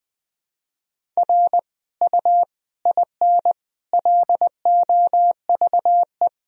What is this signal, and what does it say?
Morse code, heard as a keyed tone, about 19 wpm, 700 Hz. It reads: R U IN LOVE
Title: Sentences from Top 400 Words and Abbreviations 20wpm